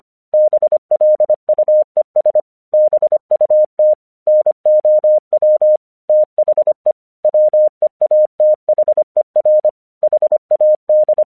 Morse code, heard as a keyed tone, about 25 wpm, 610 Hz. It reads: BLUES BUT NOW THE WEATHER HAD
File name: 1 English 25wpm / AQC15